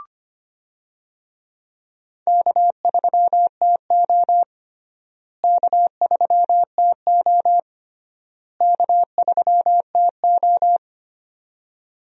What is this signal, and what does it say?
Morse code, heard as a keyed tone, about 25 wpm, 700 Hz. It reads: K3TO K3TO K3TO
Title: USA Call Signs - 4 Characters - Repeated 3x - 25wpm